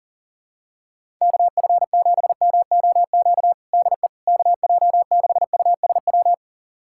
Morse code, heard as a keyed tone, about 37 wpm, 690 Hz. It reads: KF7MOQ DE KJ6USW
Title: USA Callsign DE Callsign - 6 Characters - 40wpm